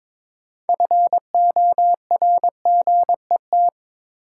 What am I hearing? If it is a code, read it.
Morse code, heard as a keyed tone, about 22 wpm, 700 Hz. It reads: FORGET
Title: Sentences from Top 1200 Words - New Word Drill Alt - 22wpm